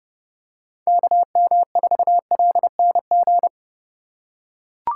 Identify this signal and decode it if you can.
Morse code, heard as a keyed tone, about 30 wpm, 700 Hz. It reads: KM4LNG
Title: USA Call Signs - 6 Characters - 3x Character Spacing - 30wpm